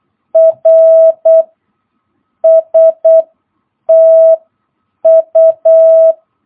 Morse code, heard as a keyed tone, about 8 wpm, 650 Hz. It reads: R S T U
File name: jh8cbh / 2022cw